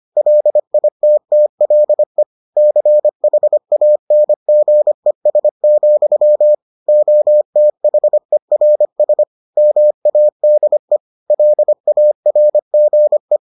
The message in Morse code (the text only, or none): LITTLE CHANGES, OTHERS MADE LARGE